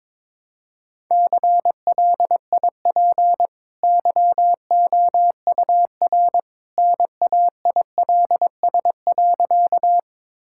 Morse code, heard as a keyed tone, about 22 wpm, 690 Hz. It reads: CLIP YOUR NAILS.
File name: practice / 22wpm / Easy Sentences with Hints - 22wpm